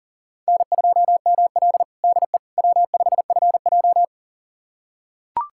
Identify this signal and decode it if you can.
Morse code, heard as a keyed tone, about 40 wpm, 700 Hz. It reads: N2ML DE W5FJ
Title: USA Callsign DE Callsign - 4 Characters - 40wpm